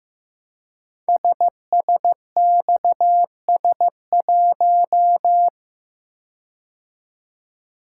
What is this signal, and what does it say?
Morse code, heard as a keyed tone, about 15 wpm, 700 Hz. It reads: SSXS1
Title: Code Groups - 5 Letter:Numbers - 15wpm